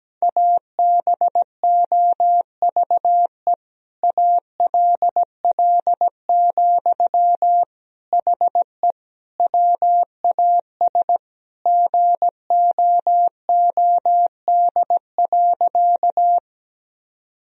Morse code, heard as a keyed tone, about 17 wpm, 700 Hz. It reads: ABOVE ALL, HE WAS GOOD.
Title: Sentences from Top 1000 Words - Review Part 2 - 17wpm